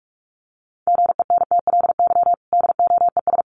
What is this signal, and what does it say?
Morse code, heard as a keyed tone, about 44 wpm, 700 Hz. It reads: GENTLY DOES
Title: Sentences from Top 1500 Words 45wpm